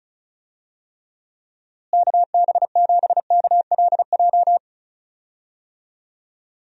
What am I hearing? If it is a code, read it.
Morse code, heard as a keyed tone, about 36 wpm, 700 Hz. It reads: KB7KLJ
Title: USA Call Signs - 6 Characters - Repeated 2x - 35wpm